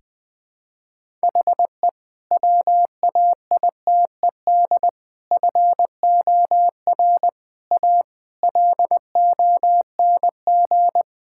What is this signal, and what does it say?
Morse code, heard as a keyed tone, about 21 wpm, 700 Hz. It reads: HE WAITED FOR A LONG
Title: Sentences with 5-Word Phrases - 20wpm